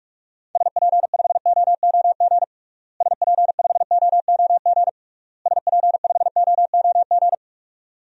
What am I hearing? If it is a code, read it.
Morse code, heard as a keyed tone, about 46 wpm, 700 Hz. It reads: SP5OOG SP5OOG SP5OOG
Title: DXCC-100 Call Signs - 6 Characters - Repeated 3x - 45wpm